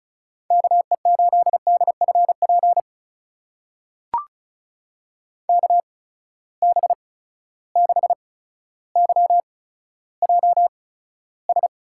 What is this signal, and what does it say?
Morse code, heard as a keyed tone, about 36 wpm, spaced as for 15 wpm, 700 Hz. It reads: KE8DFP KB6YJS
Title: USA Call Signs - 6 Characters - 8x Character Spacing - 35wpm